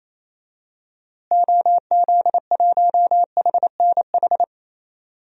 Morse code, heard as a keyed tone, about 28 wpm, 700 Hz. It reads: OZ1HNH
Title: DXCC-100 Call Signs - 6 Characters - Repeated 3x - 28wpm